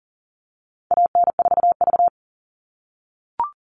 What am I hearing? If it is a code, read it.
Morse code, heard as a keyed tone, about 40 wpm, 700 Hz. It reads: AN4V
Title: All Cut Numbers - 4-Digit 40wpm